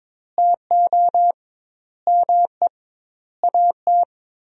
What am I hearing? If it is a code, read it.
Morse code, heard as a keyed tone, about 22 wpm, 700 Hz. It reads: TO ME AT
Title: 2x Word-Spacing - Sentences from Top 500 Words - 22wpm